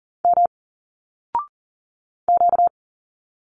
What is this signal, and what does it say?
Morse code, heard as a keyed tone, about 40 wpm, 700 Hz. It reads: M Q
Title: Single Letters 40wpm